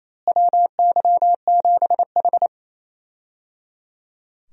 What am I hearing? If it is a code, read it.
Morse code, heard as a keyed tone, about 27 wpm, 700 Hz. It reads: WY7H
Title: USA Call Signs - 4 Characters - Repeated 3x - 28wpm